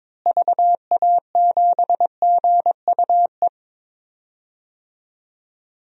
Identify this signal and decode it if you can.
Morse code, heard as a keyed tone, about 22 wpm, 700 Hz. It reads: VA7GUE
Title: DXCC-100 Call Signs - 6 Characters - Repeated 3x - 22wpm